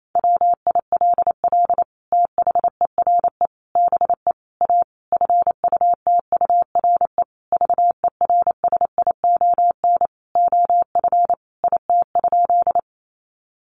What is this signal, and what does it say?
Morse code, heard as a keyed tone, about 28 wpm, 700 Hz. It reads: WILL THERE BE A FUTURE VERSION OF IT?
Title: Sentences from Top 1400 Words 28wpm